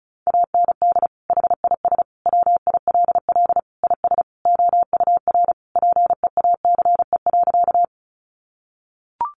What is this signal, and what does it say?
Morse code, heard as a keyed tone, about 34 wpm, 700 Hz. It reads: AND HIS WILL IS OUR PEACE.
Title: Sentences from Top 1500 Words 35wpm